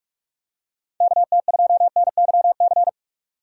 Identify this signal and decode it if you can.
Morse code, heard as a keyed tone, about 46 wpm, 700 Hz. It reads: KT2NYC
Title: USA Call Signs - 3x Character Spacing - 45wpm